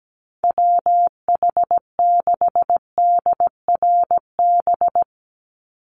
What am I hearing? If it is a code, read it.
Morse code, heard as a keyed tone, about 17 wpm, 700 Hz. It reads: WH6DRB
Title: USA Call Signs 17wpm